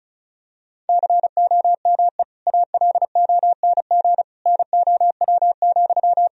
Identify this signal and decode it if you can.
Morse code, heard as a keyed tone, about 34 wpm, 700 Hz. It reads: COME ALONG NOW,
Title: Sentences from Top 1000 Words - Review Part 4 - 35wpm